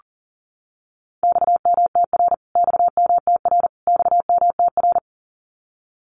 Morse code, heard as a keyed tone, about 38 wpm, 700 Hz. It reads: XMTR XMTR XMTR
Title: Abbreviations Repeated 3x - 40wpm